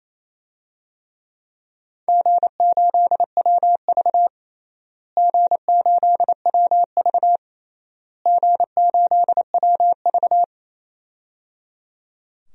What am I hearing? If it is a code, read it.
Morse code, heard as a keyed tone, about 28 wpm, 700 Hz. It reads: G8WV G8WV G8WV
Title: DXCC-100 Call Signs - 4 Characters - Repeated 3x - 28wpm